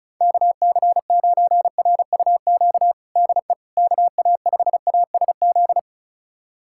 Morse code, heard as a keyed tone, about 36 wpm, 700 Hz. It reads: KC9RUQ DE KA5ASZ
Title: USA Callsign DE Callsign - 6 Characters - 35wpm